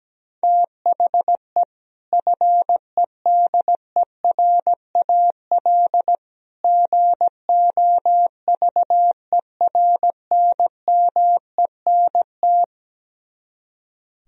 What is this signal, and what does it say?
Morse code, heard as a keyed tone, about 17 wpm, 700 Hz. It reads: THE FEDERAL GOVERNMENT
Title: Speed-Racing - Sets of 3 Words 17wpm